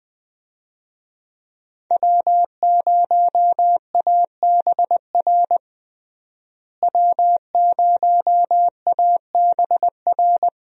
Morse code, heard as a keyed tone, about 20 wpm, 700 Hz. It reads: W0ABR W0ABR
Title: USA Call Signs - 5 Characters - Repeated 3x - 20wpm